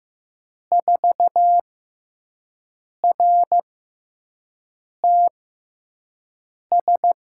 Morse code, heard as a keyed tone, about 15 wpm, 700 Hz. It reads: 4 R T S
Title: USA Call Signs - 5 Characters - 6x Character Spacing - 15wpm